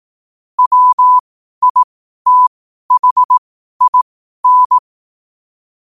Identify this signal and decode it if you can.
Morse code, heard as a keyed tone, about 18 wpm, spaced as for 12 wpm, 1000 Hz. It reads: WITHIN
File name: qotdfiles / 12wpm